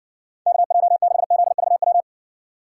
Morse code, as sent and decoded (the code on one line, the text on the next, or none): -.- .--- -.... --.. ...- .--
KJ6ZVW